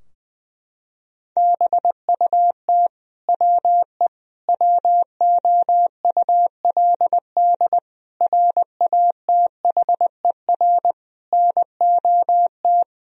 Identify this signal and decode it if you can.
Morse code, heard as a keyed tone, about 20 wpm, 700 Hz. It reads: BUT WE WOULD RATHER NOT
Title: Sentences from Top 1000 Words - Review Part 4 - 20wpm